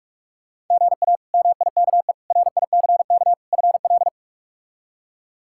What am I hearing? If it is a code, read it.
Morse code, heard as a keyed tone, about 46 wpm, 700 Hz. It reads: GA MIKE RICK FL
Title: K1USN SST QSO Practice - 45wpm